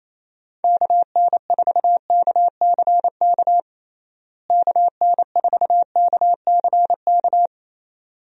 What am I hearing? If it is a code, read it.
Morse code, heard as a keyed tone, about 28 wpm, 700 Hz. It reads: KN4KCK KN4KCK
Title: USA Call Signs - 6 Characters - Repeated 2x - 28wpm